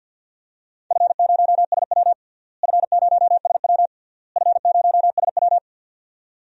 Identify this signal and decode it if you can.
Morse code, heard as a keyed tone, about 51 wpm, 700 Hz. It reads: F0SW F0SW F0SW
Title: DXCC-100 Call Signs - 4 Characters - Repeated 3x - 50wpm